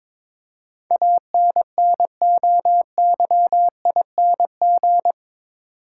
Morse code, heard as a keyed tone, about 22 wpm, 700 Hz. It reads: ANNOYING
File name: practice / 22wpm / Eight Letter Words - 22wpm